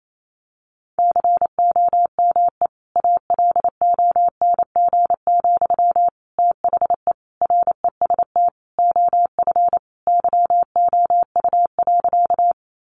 Morse code, heard as a keyed tone, about 28 wpm, 700 Hz. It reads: COME ALONG, THE REST OF YOU.
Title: Sentences from Top 700 Words 28wpm